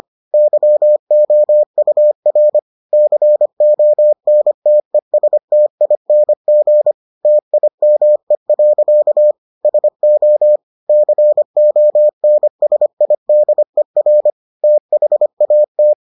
Morse code, heard as a keyed tone, about 25 wpm, 600 Hz. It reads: YOUR CONTESTING TIME. SO CONSIDER THAT